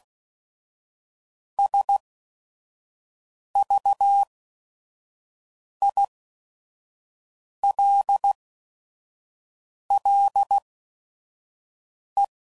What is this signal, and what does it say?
Morse code, heard as a keyed tone, about 16 wpm, spaced as for 5 wpm, 800 Hz. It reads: SVILLE
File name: CW Code Groups @5WPM / fromSM's / Cities